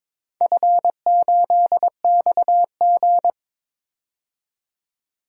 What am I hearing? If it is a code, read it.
Morse code, heard as a keyed tone, about 22 wpm, 700 Hz. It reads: F8XG
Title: DXCC-100 Call Signs - 4 Characters - 22wpm